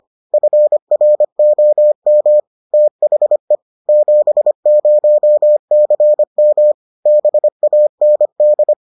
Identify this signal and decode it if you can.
Morse code, heard as a keyed tone, about 25 wpm, 600 Hz. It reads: FROM THE 70CM BAND